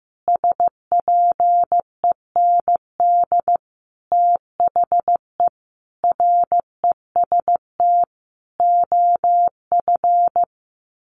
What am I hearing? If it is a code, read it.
Morse code, heard as a keyed tone, about 14 wpm, 700 Hz. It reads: SPEND THE REST OF
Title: Sets of 5 Words Encore2 15wpm